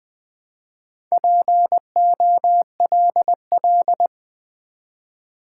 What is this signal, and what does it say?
Morse code, heard as a keyed tone, about 20 wpm, 700 Hz. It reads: POLL
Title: Build A Word - 1 Letter At A Time - 4-Letter Words - 20wpm